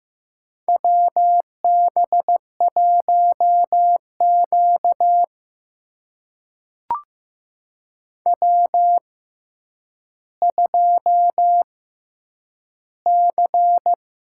Morse code, heard as a keyed tone, about 15 wpm, spaced as for 9 wpm, 700 Hz. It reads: WB1Q W2C
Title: USA Call Signs - 4 Characters - 6x Character Spacing - 15wpm